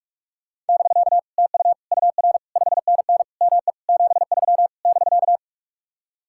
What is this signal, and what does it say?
Morse code, heard as a keyed tone, about 49 wpm, 700 Hz. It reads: <BK> TU UR 5NN ME 73 <BK>